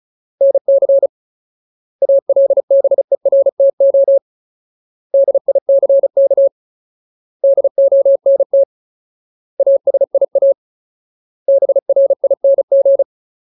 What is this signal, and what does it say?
Morse code, heard as a keyed tone, about 34 wpm, 550 Hz. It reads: NC ALBERTO DICK DONT ASIA BRING
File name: Ultimate Word List / 35WPM Ultimate